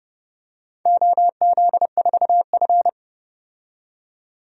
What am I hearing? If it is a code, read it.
Morse code, heard as a keyed tone, about 31 wpm, 700 Hz. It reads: OZ4F
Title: DXCC-100 Call Signs - 4 Characters - 30wpm